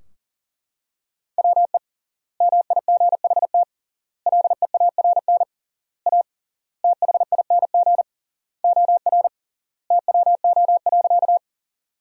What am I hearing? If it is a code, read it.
Morse code, heard as a keyed tone, about 40 wpm, 700 Hz. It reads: WE MIGHT LEARN A THING OR TWO.
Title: 3x Word-Spacing - Sentences from Top 300 Words - 40wpm